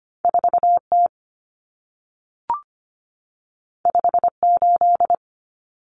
Morse code, heard as a keyed tone, about 25 wpm, 700 Hz. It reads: 4T 58
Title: Common Cut Numbers - 2-Digit 25wpm